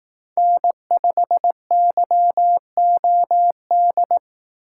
Morse code, heard as a keyed tone, about 18 wpm, 700 Hz. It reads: N5YOD
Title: Speed Racing - USA 1x3 Call Signs 15wpm